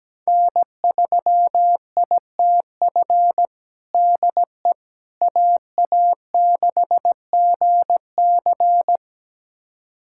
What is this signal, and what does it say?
Morse code, heard as a keyed tone, about 17 wpm, 700 Hz. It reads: N3ITF DE AA6GC
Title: USA Callsign DE Callsign - 5 Characters - 17wpm